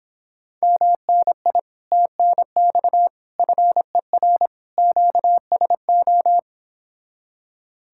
Morse code, heard as a keyed tone, about 29 wpm, 700 Hz. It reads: MNI TNX FER QSO